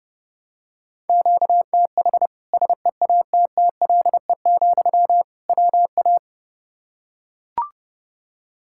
Morse code, heard as a keyed tone, about 30 wpm, 700 Hz. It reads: QTH SEATTLE, WA
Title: Practice QSO - Exchange 1 - Beginning - 30wpm